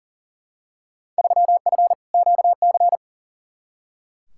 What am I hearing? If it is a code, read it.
Morse code, heard as a keyed tone, about 40 wpm, 700 Hz. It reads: 3F QC
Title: Field Day Exchanges - 40wpm